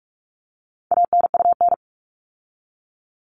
Morse code, heard as a keyed tone, about 44 wpm, 700 Hz. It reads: ANUN
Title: All Cut Numbers - 4-Digit 45wpm